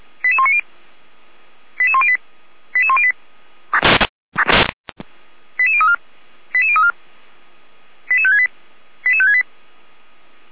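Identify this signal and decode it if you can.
Five-tone selective call (ZVEI), heard as quick runs of five tones, tones of about 70 ms. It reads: 88190 88108 88108 88034 88034 88568 88568
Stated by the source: ZVEI five-tone call 88106 received by SDR.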